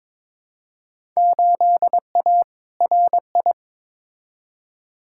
Morse code, heard as a keyed tone, about 22 wpm, 700 Hz. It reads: 8A RI
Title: Field Day Exchanges Encore - 22wpm